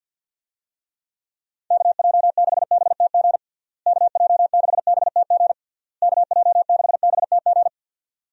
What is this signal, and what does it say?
Morse code, heard as a keyed tone, about 50 wpm, 700 Hz. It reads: KJ6BTG KJ6BTG KJ6BTG
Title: USA Call Signs - 6 Characters - Repeated 3x - 50wpm